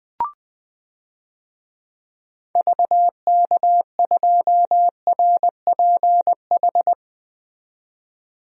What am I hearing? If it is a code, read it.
Morse code, heard as a keyed tone, about 20 wpm, 700 Hz. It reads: VK2RPH
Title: DXCC-100 Call Signs - 6 Characters - 20wpm